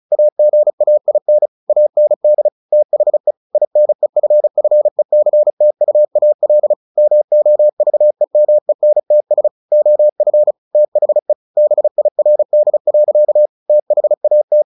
Morse code, heard as a keyed tone, about 34 wpm, 600 Hz. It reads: AGAIN AND THE INEFFECTUAL MOVEMENTS OF THE BIRD. THAT
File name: the old man and the sea / the old man and the sea 1 35wpm